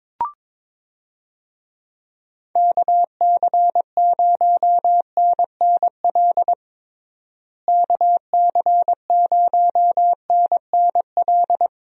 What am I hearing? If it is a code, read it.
Morse code, heard as a keyed tone, about 22 wpm, 700 Hz. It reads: KC0NNL KC0NNL
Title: USA Call Signs - 6 Characters - Repeated 3x - 22wpm